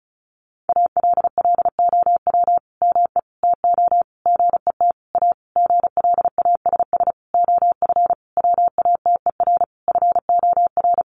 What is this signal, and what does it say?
Morse code, heard as a keyed tone, about 34 wpm, 700 Hz. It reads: ALLOW ME TO GET A GLASS OF WATER FOR
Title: Sentences from Top 1300 Words 35wpm